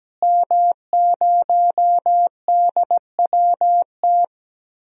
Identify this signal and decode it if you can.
Morse code, heard as a keyed tone, about 17 wpm, 700 Hz. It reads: M0DWT
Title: DXCC-100 Call Signs - 5 Characters - Repeated 3x - 17wpm